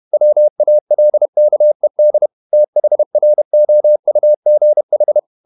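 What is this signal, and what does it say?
Morse code, heard as a keyed tone, about 31 wpm, 600 Hz. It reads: WALKED THROUGH
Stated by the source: Morse code at 30 wpm, the novel Never by Ken Follett in English, from CW med Gnister.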